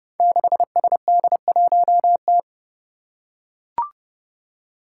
Morse code, heard as a keyed tone, about 30 wpm, 700 Hz. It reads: 6SD1T
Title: Five Challenging Letter-Number Combinations 30wpm